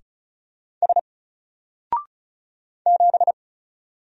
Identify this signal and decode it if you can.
Morse code, heard as a keyed tone, about 34 wpm, 700 Hz. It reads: S 7
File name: Single-Letter-Number-35wpm